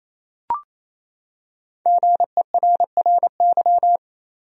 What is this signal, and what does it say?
Morse code, heard as a keyed tone, about 28 wpm, 700 Hz. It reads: GERRY
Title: USA Common Names - Max 5 Letters - 28wpm